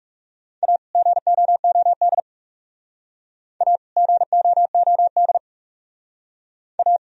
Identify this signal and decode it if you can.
Morse code, heard as a keyed tone, about 37 wpm, 700 Hz. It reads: A GOOD A GOOD A
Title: Speed-Racing - Sets of 2 Words 30wpm